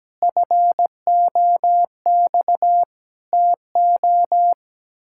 Morse code, heard as a keyed tone, about 17 wpm, 700 Hz. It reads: FOX TO
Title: Pangram Challenge v2 - 17wpm